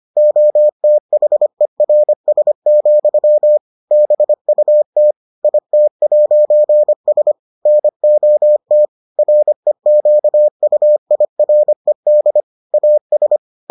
Morse code, heard as a keyed tone, about 25 wpm, 600 Hz. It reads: OTHERS, BUT IT'S NOT REQUIRED AS